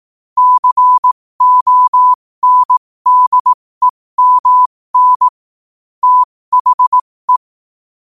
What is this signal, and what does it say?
Morse code, heard as a keyed tone, about 18 wpm, spaced as for 15 wpm, 1000 Hz. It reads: CONDEMN THE
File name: qotdfiles / 15wpm